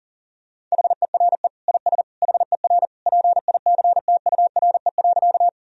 Morse code, heard as a keyed tone, about 40 wpm, 700 Hz. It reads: HERE IS HER PICTURE.
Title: Sentences from Top 1000 Words - Review Part 1 - 40wpm